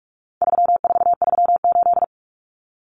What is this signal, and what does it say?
Morse code, heard as a keyed tone, about 44 wpm, 700 Hz. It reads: 3438
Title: Numbers - 4-Digit 45wpm